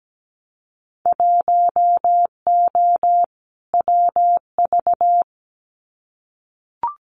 Morse code, heard as a keyed tone, about 17 wpm, 700 Hz. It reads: 1O WV
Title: Winter Field Day Exchanges Encore 17wpm